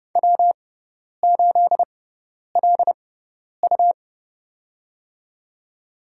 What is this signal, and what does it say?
Morse code, heard as a keyed tone, about 29 wpm, spaced as for 10 wpm, 700 Hz. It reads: W8LU
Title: USA Call Signs - 4 Characters - 6x Character Spacing - 30wpm